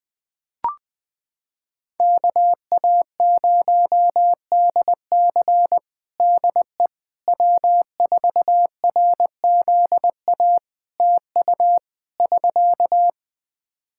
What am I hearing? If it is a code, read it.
Morse code, heard as a keyed tone, about 21 wpm, 700 Hz. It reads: KA0DC DE W4RZA TU <SK>
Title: Practice QSO - Exchange 4 - Ending - 20wpm